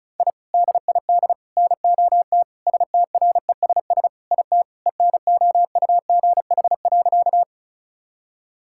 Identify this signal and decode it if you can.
Morse code, heard as a keyed tone, about 36 wpm, 700 Hz. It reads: I DID NOT STRESS IT ENOUGH.
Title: Sentences from Top 1000 Words - Review Part 6 - 35wpm